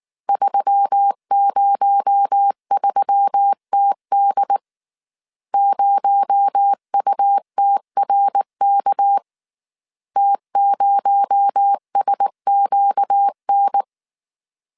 Morse code, heard as a keyed tone, about 19 wpm, 800 Hz. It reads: 303TD 0UTRK T0SQN
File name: CW / DET / 1 6 1500C S A 18wpm 2255